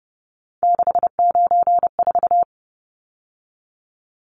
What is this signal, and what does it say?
Morse code, heard as a keyed tone, about 30 wpm, 700 Hz. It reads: 694
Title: Numbers - 3-Digit 30wpm